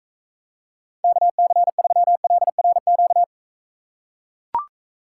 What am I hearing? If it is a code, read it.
Morse code, heard as a keyed tone, about 43 wpm, 700 Hz. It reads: KC3LRQ